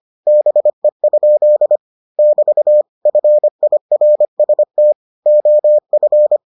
Morse code, heard as a keyed tone, about 25 wpm, 600 Hz. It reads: BE? <BT> FIRST OF